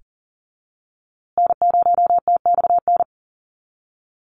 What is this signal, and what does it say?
Morse code, heard as a keyed tone, about 40 wpm, 700 Hz. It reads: N0TXN
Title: USA Call Signs Repeated 2x 40wpm